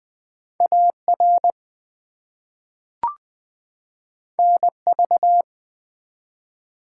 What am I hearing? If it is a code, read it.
Morse code, heard as a keyed tone, about 20 wpm, 700 Hz. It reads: AR NV
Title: ARRL Sections v2 - 20wpm